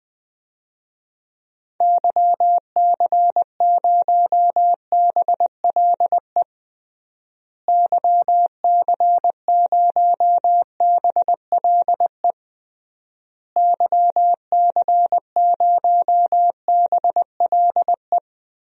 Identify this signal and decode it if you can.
Morse code, heard as a keyed tone, about 20 wpm, 700 Hz. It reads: YC0BLE YC0BLE YC0BLE
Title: DXCC-100 Call Signs - 6 Characters - Repeated 3x - 20wpm